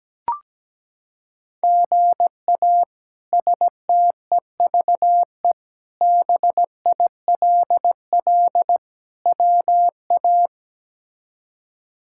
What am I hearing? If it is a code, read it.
Morse code, heard as a keyed tone, about 17 wpm, 700 Hz. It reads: GA STEVE BILL WA
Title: K1USN SST QSO Practice - 17wpm